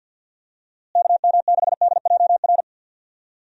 Morse code, heard as a keyed tone, about 52 wpm, 700 Hz. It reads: KM6DJR
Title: USA Call Signs - 6 Characters - Repeated 3x - 50wpm